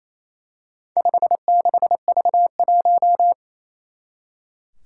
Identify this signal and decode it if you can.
Morse code, heard as a keyed tone, about 28 wpm, 700 Hz. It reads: 56V1
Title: Four Challenging Letter-Number Combinations 28wpm